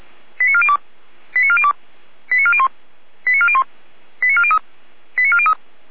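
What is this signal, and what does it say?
Five-tone selective call (ZVEI), heard as quick runs of five tones, tones of about 70 ms. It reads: 89402 89402 89401 89401 89403 89403
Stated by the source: Five-tone ZVEI call 89175, received by SDR.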